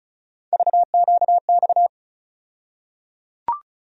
Morse code, heard as a keyed tone, about 34 wpm, 700 Hz. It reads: VQX
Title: USA Call Sign Postfixes - 3 Letters 35wpm